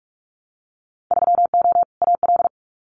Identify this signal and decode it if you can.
Morse code, heard as a keyed tone, about 46 wpm, 700 Hz. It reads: 3O AL
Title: Winter Field Day Exchanges 45wpm